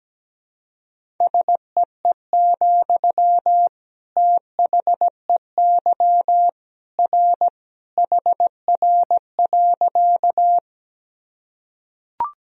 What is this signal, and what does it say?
Morse code, heard as a keyed tone, about 17 wpm, 700 Hz. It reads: SEE, THEY R HR.
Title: Sentences from Top 200 Words and Abbreviations 17wpm